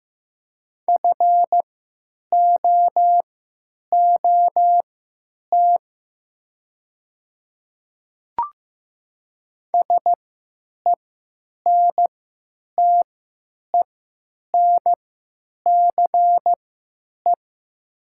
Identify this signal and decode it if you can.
Morse code, heard as a keyed tone, about 15 wpm, spaced as for 9 wpm, 700 Hz. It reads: FOOT SENTENCE
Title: 3x Character Spacing - Top 500 Words - 15wpm